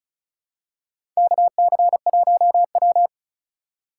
Morse code, heard as a keyed tone, about 34 wpm, 700 Hz. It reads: KC1W
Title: USA Call Signs - 4 Characters - 4x Character Spacing - 35wpm